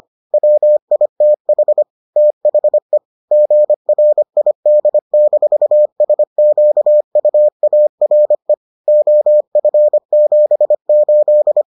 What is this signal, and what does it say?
Morse code, heard as a keyed tone, about 25 wpm, 600 Hz. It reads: WITH THE GRID-SQUARE OF78